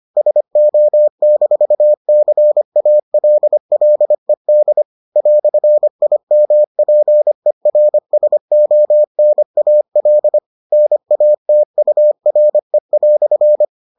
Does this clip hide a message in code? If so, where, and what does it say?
Yes: Morse code from 0.2 to 13.6 s, heard as a keyed tone, about 25 wpm, 600 Hz. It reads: SO-CALLED "IMPERSONAL NATURE"